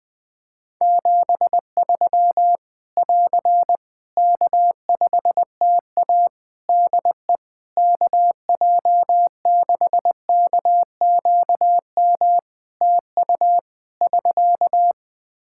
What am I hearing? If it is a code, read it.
Morse code, heard as a keyed tone, about 20 wpm, 700 Hz. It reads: 73 <AR> K5TA DE KJ6KQM TU <SK>
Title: Practice QSO - Exchange 4 - Ending - 20wpm